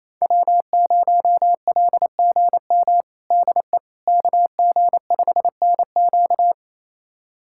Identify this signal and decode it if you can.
Morse code, heard as a keyed tone, about 28 wpm, 700 Hz. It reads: W0LGM DE KG5NQ
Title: USA Callsign DE Callsign - 5 Characters - 28wpm